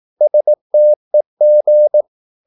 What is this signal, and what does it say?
Morse code, heard as a keyed tone, about 18 wpm, 600 Hz. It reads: STEG
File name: Korte ord DK 18wpm